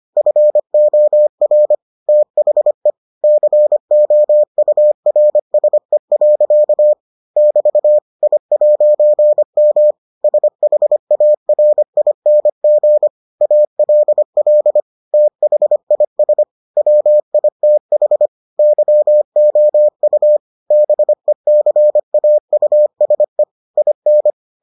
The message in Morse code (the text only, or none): FOR THE COURSE. <BT> I'M SHARING ALL THIS WITH YOU BECAUSE IN